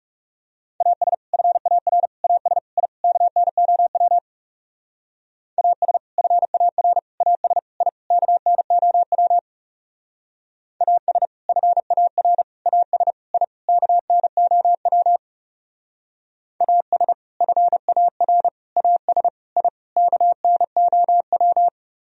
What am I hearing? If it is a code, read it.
Morse code, heard as a keyed tone, about 37 wpm, 700 Hz. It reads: AS FAR AS I KNOW AS FAR AS I KNOW AS FAR AS I KNOW AS FAR AS I KNOW
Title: Speed-Racing - Sets of 5 Words 30wpm